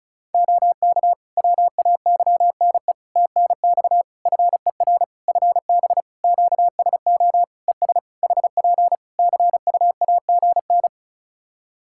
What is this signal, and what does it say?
Morse code, heard as a keyed tone, about 36 wpm, 700 Hz. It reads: OK WAYNE TNX FER FB QSO ES HP CUAGN
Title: Practice QSO - Exchange 4 - Ending - 35wpm